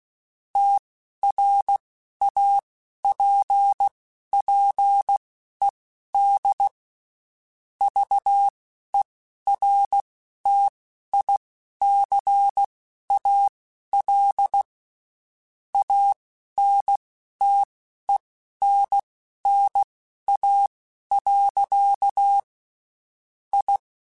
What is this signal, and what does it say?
Morse code, heard as a keyed tone, about 16 wpm, spaced as for 11 wpm, 800 Hz. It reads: TRAPPED VERTICAL ANTENNA. I
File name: (spaced) / QSOs Spaced-mp3s / kc7haa @15 sp